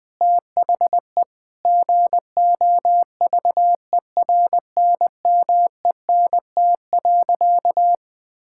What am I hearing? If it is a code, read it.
Morse code, heard as a keyed tone, about 20 wpm, 700 Hz. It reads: THE GOVERNMENT.
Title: Sentences from Top 1000 Words - Review Part 5 - 20wpm